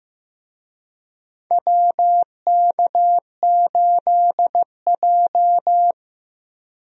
Morse code, heard as a keyed tone, about 15 wpm, 700 Hz. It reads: WK8J
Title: USA Call Signs - 4 Characters - Repeated 2x - 15wpm